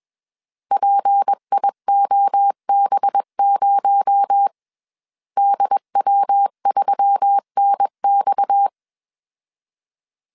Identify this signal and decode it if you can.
Morse code, heard as a keyed tone, about 21 wpm, 800 Hz. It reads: PIOB0 DW3NX
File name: CW / DET / 1 6 1500C S A 20wpm 2048